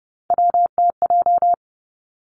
Morse code, heard as a keyed tone, about 30 wpm, 700 Hz. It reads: WTJ
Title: Three Challenging Letter-Number Combinations 30wpm